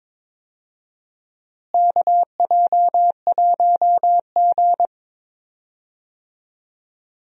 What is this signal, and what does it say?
Morse code, heard as a keyed tone, about 22 wpm, 700 Hz. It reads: KJ1G
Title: USA Call Signs - 4 Characters - Repeated 3x - 22wpm